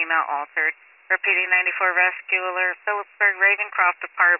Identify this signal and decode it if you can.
Emergency Alert System (SAME) tones, heard as harsh buzzing bursts, with speech. It is heard from 0.5 to 3.6 s.